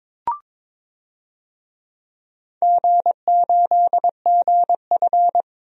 Morse code, heard as a keyed tone, about 22 wpm, 700 Hz. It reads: G8GF
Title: DXCC-100 Call Signs - 4 Characters - 22wpm